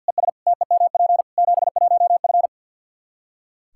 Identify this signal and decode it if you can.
Morse code, heard as a keyed tone, about 46 wpm, 700 Hz. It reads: ES TEMP 71F
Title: K1USN SST QSO Practice 50wpm